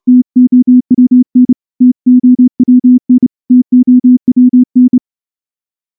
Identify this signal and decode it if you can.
Morse code, heard as a keyed tone, about 30 wpm, 250 Hz. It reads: TOWN TOWN TOWN
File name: morse-training-30WPM